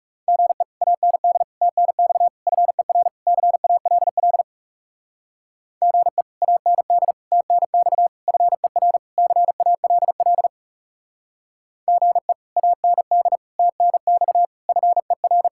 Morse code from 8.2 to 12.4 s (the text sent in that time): FER CALL GE